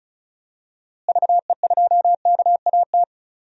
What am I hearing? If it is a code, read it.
Morse code, heard as a keyed tone, about 36 wpm, 700 Hz. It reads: VE2KAT
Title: DXCC-100 Call Signs - 6 Characters - Repeated 2x - 35wpm